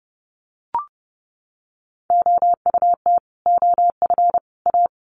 Morse code, heard as a keyed tone, about 30 wpm, 700 Hz. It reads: OUT OF A
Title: Sets of 3 Words 30wpm